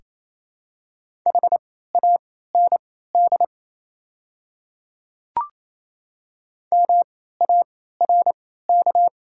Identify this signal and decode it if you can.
Morse code, heard as a keyed tone, about 27 wpm, spaced as for 16 wpm, 700 Hz. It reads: HAND MARK
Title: 3x Character Spacing - Top 300 Words - 28wpm